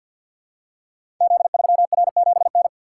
Morse code, heard as a keyed tone, about 50 wpm, 700 Hz. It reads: Z3R7N
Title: Code Groups - 5 Letter:Numbers - 50wpm